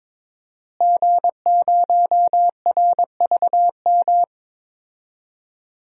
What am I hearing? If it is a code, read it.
Morse code, heard as a keyed tone, about 21 wpm, 700 Hz. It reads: G0RVM
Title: DXCC-100 Call Signs - 5 Characters - Repeated 3x - 22wpm